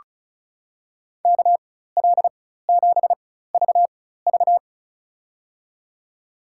Morse code, heard as a keyed tone, about 36 wpm, spaced as for 16 wpm, 700 Hz. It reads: KL7VV
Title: USA Call Signs - 5 Characters - 4x Character Spacing - 35wpm